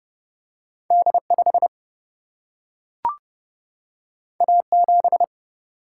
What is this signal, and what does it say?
Morse code, heard as a keyed tone, about 29 wpm, 700 Hz. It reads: D5 A7
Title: Single-Letter Single-Number 30wpm